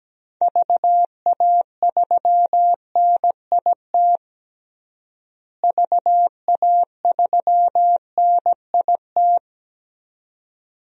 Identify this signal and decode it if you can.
Morse code, heard as a keyed tone, about 17 wpm, 700 Hz. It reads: VA3NIT VA3NIT
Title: DXCC-100 Call Signs - 6 Characters - Repeated 2x - 17wpm